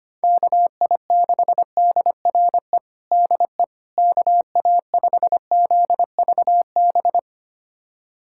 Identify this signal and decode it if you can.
Morse code, heard as a keyed tone, about 25 wpm, 700 Hz. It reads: KI6DRE DE KA5ZVB
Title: USA Callsign DE Callsign 25wpm